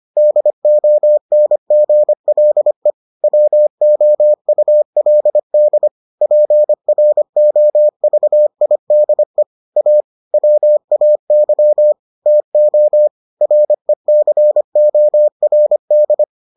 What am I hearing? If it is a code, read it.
Morse code, heard as a keyed tone, about 25 wpm, 600 Hz. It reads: DONGLE WOULD PROVIDE A WAY TO RECORD